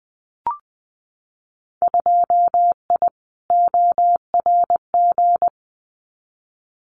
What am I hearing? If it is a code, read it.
Morse code, heard as a keyed tone, about 19 wpm, 700 Hz. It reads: 2I ORG
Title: Winter Field Day Exchanges Encore 20wpm